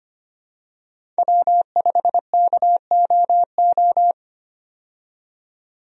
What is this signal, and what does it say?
Morse code, heard as a keyed tone, about 25 wpm, 700 Hz. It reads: W5KOO
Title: USA Call Signs - 5 Characters - 25wpm